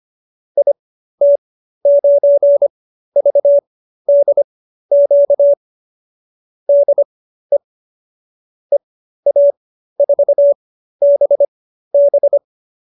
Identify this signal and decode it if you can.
Morse code, heard as a keyed tone, about 25 wpm, spaced as for 13 wpm, 590 Hz. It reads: IT9VDQ DE EA4BB